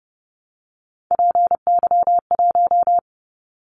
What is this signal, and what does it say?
Morse code, heard as a keyed tone, about 30 wpm, 700 Hz. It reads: PY1
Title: Callsign Prefixes for Top 100 DXCC - 30wpm